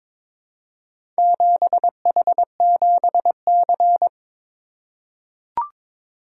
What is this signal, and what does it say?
Morse code, heard as a keyed tone, about 22 wpm, 700 Hz. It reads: 7H7C
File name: practice / 22wpm / Four Letter-Number Combinations - 22wpm